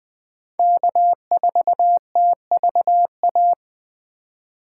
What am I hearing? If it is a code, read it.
Morse code, heard as a keyed tone, about 24 wpm, 700 Hz. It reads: K4TVA
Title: Speed Racing - USA 1x3 Call Signs 20wpm